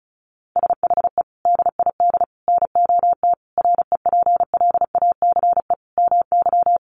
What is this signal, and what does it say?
Morse code, heard as a keyed tone, about 34 wpm, 700 Hz. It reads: SHE DID NOT REPLACE MY
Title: Sentences from Top 1400 Words 35wpm